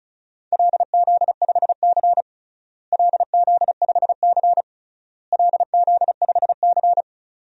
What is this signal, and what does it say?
Morse code, heard as a keyed tone, about 36 wpm, 700 Hz. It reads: LZ5C LZ5C LZ5C
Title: DXCC-100 Call Signs - 4 Characters - Repeated 3x - 35wpm